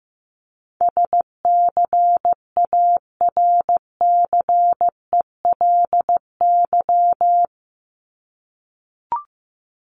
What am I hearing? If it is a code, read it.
Morse code, heard as a keyed tone, about 15 wpm, 700 Hz. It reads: SCARCELY
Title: LY Suffix - Top 100 Words with Base Words 15wpm